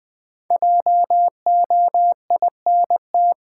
Morse code, heard as a keyed tone, about 20 wpm, 700 Hz. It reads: JOINT
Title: Sentences from Top 1300 Words - New Word Drill Alt - 20wpm